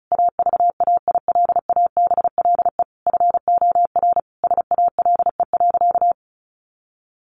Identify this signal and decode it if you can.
Morse code, heard as a keyed tone, about 34 wpm, 700 Hz. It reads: AVAILABLE FOR SALE.
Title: Sentences from Top 1400 Words 35wpm